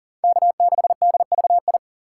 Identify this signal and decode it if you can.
Morse code, heard as a keyed tone, about 40 wpm, 700 Hz. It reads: K6DVI
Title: USA Call Signs - 3x Character Spacing - 40wpm